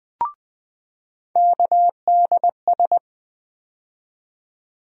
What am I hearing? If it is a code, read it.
Morse code, heard as a keyed tone, about 20 wpm, 700 Hz. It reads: KDS
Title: USA Call Sign Postfixes - 3 Letters 20wpm